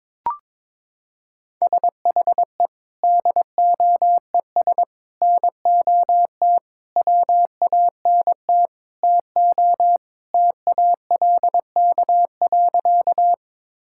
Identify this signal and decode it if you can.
Morse code, heard as a keyed tone, about 22 wpm, 700 Hz. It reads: SHE DOES NOT WANT TO TALK.
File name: practice / 22wpm / Sentences with 4-Word Phrases - 22wpm